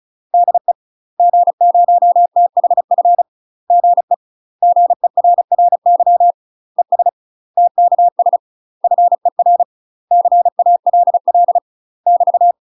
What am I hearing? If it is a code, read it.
Morse code, heard as a keyed tone, about 36 wpm, 700 Hz. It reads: DE G0THF GE GERRY ES TKS FER CALL <BT>
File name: UK MASTER QSOs TRAINING / 35WPM / UK MASTER QSOs 0004